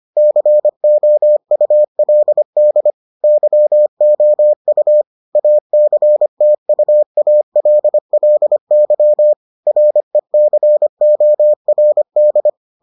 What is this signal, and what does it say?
Morse code, heard as a keyed tone, about 25 wpm, 600 Hz. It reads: COULD YOU ACTUALLY RECORD